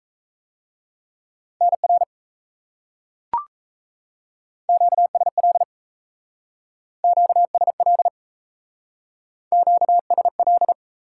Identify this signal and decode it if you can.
Morse code, heard as a keyed tone, about 34 wpm, 700 Hz. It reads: NR QSL QSL QSL
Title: Speed-Racing Top 90 QSO Elements 28wpm